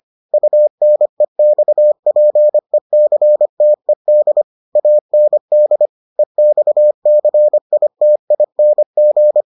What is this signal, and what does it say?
Morse code, heard as a keyed tone, about 25 wpm, 600 Hz. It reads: UNEXPECTED AND EXCITING